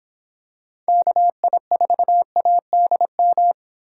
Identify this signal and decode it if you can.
Morse code, heard as a keyed tone, about 25 wpm, 700 Hz. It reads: KI4ADM